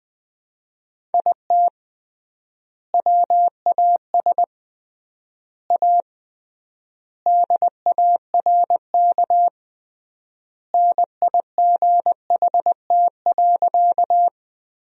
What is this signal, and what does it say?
Morse code, heard as a keyed tone, about 20 wpm, 700 Hz. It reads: IT WAS A DARK NIGHT.
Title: 3x Word-Spacing - Sentences from Top 500 Words - 20wpm